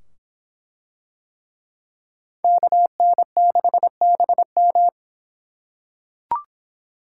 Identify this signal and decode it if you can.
Morse code, heard as a keyed tone, about 23 wpm, 700 Hz. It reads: KN6BM